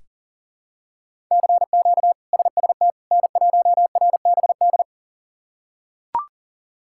Morse code, heard as a keyed tone, about 37 wpm, 700 Hz. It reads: CQ SST N1RBD
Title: K1USN SST QSO Practice - 40wpm